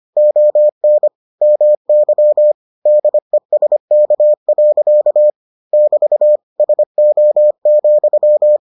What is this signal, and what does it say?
Morse code, heard as a keyed tone, about 25 wpm, 600 Hz. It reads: ON MY DESK. <BT> SO,